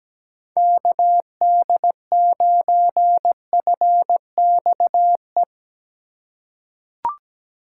Morse code, heard as a keyed tone, about 17 wpm, 700 Hz. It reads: KD9FXE
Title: USA Call Signs - 6 Characters - 2x Character Spacing - 17wpm